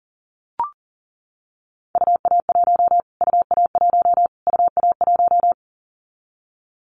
Morse code, heard as a keyed tone, about 40 wpm, 700 Hz. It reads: UA1 UA1 UA1
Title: Callsign Prefixes for Top 25 DXCC - Repeated 3x - 40wpm